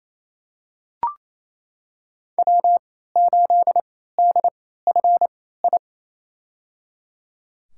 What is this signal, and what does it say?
Morse code, heard as a keyed tone, about 28 wpm, spaced as for 16 wpm, 700 Hz. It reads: W8DFI
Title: USA Call Signs - 3x Character Spacing - 28wpm